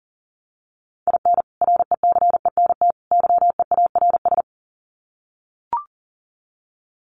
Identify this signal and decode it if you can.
Morse code, heard as a keyed tone, about 40 wpm, 700 Hz. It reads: IN RECENT YEARS
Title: Sets of 3 Words 40wpm